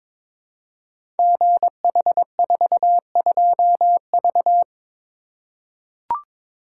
Morse code, heard as a keyed tone, about 22 wpm, 700 Hz. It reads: GH42V
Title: Code Groups - 5 Letter:Numbers - 22wpm